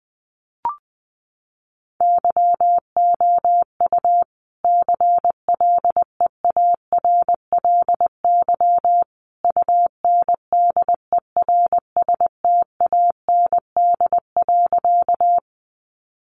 Morse code, heard as a keyed tone, about 20 wpm, 700 Hz. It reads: YOU CLEARLY UNDERSTAND.
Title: Sentences from Top 1300 Words 20wpm